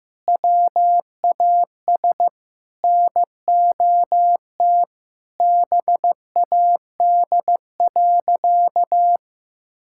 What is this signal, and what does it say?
Morse code, heard as a keyed tone, about 15 wpm, 700 Hz. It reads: WAS NOT BAD.
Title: Sentences from Top 1000 Words - Review Part 6 - 15wpm